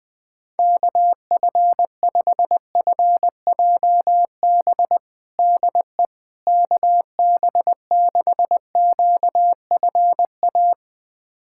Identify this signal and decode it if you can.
Morse code, heard as a keyed tone, about 20 wpm, 700 Hz. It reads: KF5FJB DE KB6QFA
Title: USA Callsign DE Callsign - 6 Characters - 20wpm